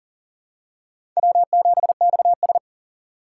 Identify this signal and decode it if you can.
Morse code, heard as a keyed tone, about 40 wpm, 700 Hz. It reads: W7XS
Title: USA Call Signs - 4 Characters - 2x Character Spacing - 40wpm